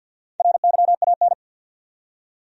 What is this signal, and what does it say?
Morse code, heard as a keyed tone, about 52 wpm, 700 Hz. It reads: RYAN
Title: USA Common Names - Max 4 Letters - 50wpm